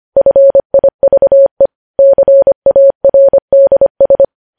Morse code, heard as a keyed tone, about 25 wpm, 550 Hz. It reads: FIVE CARDS
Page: News Headlines in Morse Code at 25 WPM